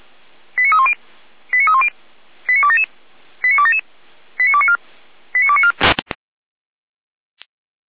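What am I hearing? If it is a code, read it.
Five-tone selective call (ZVEI), heard as quick runs of five tones, tones of about 70 ms. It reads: 89310 89310 89277 89277 89294 89294
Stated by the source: ZVEI five-tone call received by SDR.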